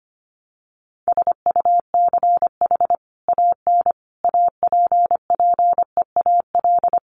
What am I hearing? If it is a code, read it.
Morse code, heard as a keyed tone, about 25 wpm, 700 Hz. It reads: SUCH AN APPEAL
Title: Sentences from Top 1300 Words 25wpm